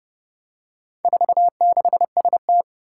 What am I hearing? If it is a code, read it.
Morse code, heard as a keyed tone, about 28 wpm, 700 Hz. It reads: 46ST
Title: Four Challenging Letter-Number Combinations 30wpm